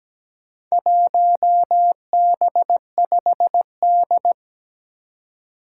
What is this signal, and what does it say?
Morse code, heard as a keyed tone, about 17 wpm, 700 Hz. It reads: 1B5D
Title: Four Challenging Letter-Number Combinations 17wpm